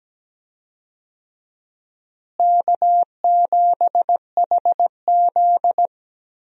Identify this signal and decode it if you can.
Morse code, heard as a keyed tone, about 17 wpm, 700 Hz. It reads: K7HZ
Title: USA Call Signs - 4 Characters - Repeated 3x - 17wpm